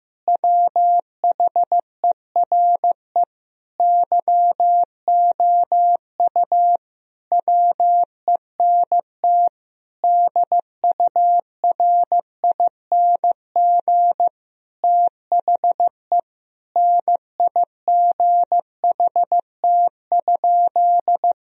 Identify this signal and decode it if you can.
Morse code, heard as a keyed tone, about 15 wpm, 700 Hz. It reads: WHERE YOU WENT DURING THE NIGHT?
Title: Sentences from Top 1000 Words - Review Part 2 - 15wpm